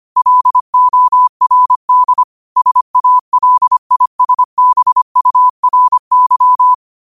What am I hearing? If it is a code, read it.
Morse code, heard as a keyed tone, about 25 wpm, 1000 Hz. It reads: LORD SALISBURY
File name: qotdfiles / 25wpm